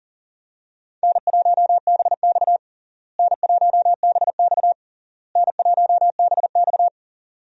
Morse code, heard as a keyed tone, about 40 wpm, 700 Hz. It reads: N1BX N1BX N1BX
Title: USA Call Signs - 4 Characters - Repeated 3x - 40wpm